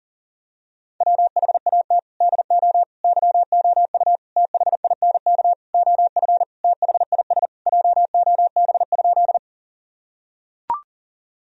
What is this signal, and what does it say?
Morse code, heard as a keyed tone, about 40 wpm, 700 Hz. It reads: WHAT DO YOU THINK OF THIS JOB?
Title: Sentences from Top 1000 Words - Review Part 3 - 40wpm